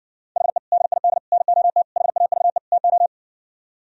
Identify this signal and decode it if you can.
Morse code, heard as a keyed tone, about 60 wpm, 700 Hz. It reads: HE DID NOT HAVE TO